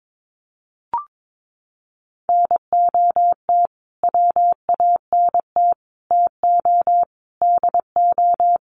Morse code, heard as a keyed tone, about 22 wpm, 700 Hz. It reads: NOT WANT TO DO
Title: Sets of 5 Words Encore 22wpm